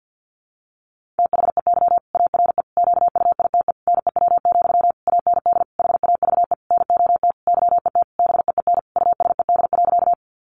Morse code, heard as a keyed tone, about 50 wpm, 700 Hz. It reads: THEY ARE QUITE NEW, AND HAVE NOT YET BEEN USED.
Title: Sentences from Top 700 Words 50wpm